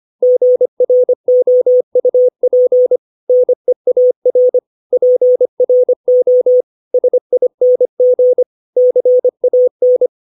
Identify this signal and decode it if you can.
Morse code, heard as a keyed tone, about 25 wpm, 500 Hz. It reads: GROUP NEAR PRO SING CAN
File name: 632 words / 632 Words 25WPM